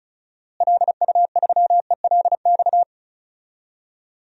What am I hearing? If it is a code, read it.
Morse code, heard as a keyed tone, about 36 wpm, 700 Hz. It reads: LU3ELX
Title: DXCC-100 Call Signs - 6 Characters - Repeated 3x - 35wpm